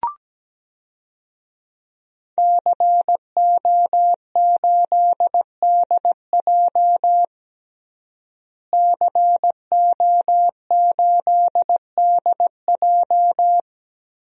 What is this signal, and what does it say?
Morse code, heard as a keyed tone, about 17 wpm, 700 Hz. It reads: CO8DJ CO8DJ
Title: DXCC-100 Call Signs - 5 Characters - Repeated 3x - 17wpm